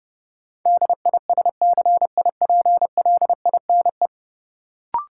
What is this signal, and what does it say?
Morse code, heard as a keyed tone, about 32 wpm, 700 Hz. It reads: DISCIPLINE
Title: Sentences from Top 900 Words - New Word Drill - 30wpm